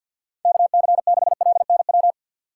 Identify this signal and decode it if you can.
Morse code, heard as a keyed tone, about 51 wpm, 700 Hz. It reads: KC6LNW
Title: USA Call Signs - 6 Characters - 6x Character Spacing - 50wpm